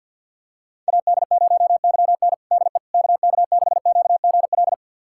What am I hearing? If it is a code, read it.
Morse code, heard as a keyed tone, about 51 wpm, 700 Hz. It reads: AD0YN DE KK6QGL
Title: USA Callsign DE Callsign 50wpm